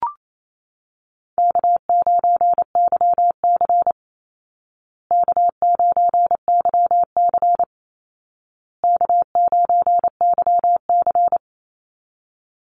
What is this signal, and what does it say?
Morse code, heard as a keyed tone, about 28 wpm, 700 Hz. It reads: K9YC K9YC K9YC
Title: Easy Call Signs Repeated 3x - 28wpm